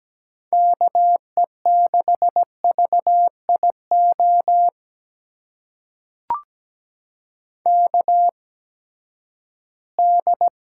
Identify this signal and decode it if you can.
Morse code, heard as a keyed tone, about 17 wpm, spaced as for 12 wpm, 700 Hz. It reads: KE6VIO KD
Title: USA Call Signs - 6 Characters - 8x Character Spacing - 17wpm